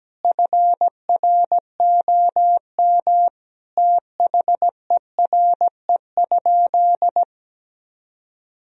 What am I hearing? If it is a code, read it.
Morse code, heard as a keyed tone, about 17 wpm, 700 Hz. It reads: FROM THERE?
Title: Sentences from Top 1000 Words - Review Part 1 - 17wpm